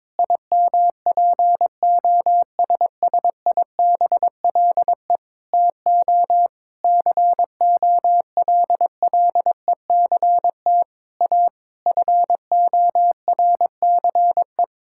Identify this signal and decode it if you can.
Morse code, heard as a keyed tone, about 21 wpm, 700 Hz. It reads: IMPOSSIBLE TO COLLECT A FORCE
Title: Sentences from Top 1600 Words 22wpm